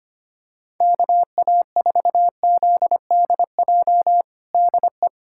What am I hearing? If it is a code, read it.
Morse code, heard as a keyed tone, about 25 wpm, 700 Hz. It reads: KA4ZDJ DE
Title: USA Callsign DE Callsign - 6 Characters - 25wpm